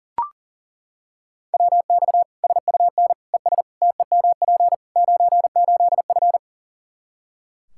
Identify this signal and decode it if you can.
Morse code, heard as a keyed tone, about 40 wpm, 700 Hz. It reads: WX SUN ES TEMP 98F
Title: K1USN SST QSO Practice 40wpm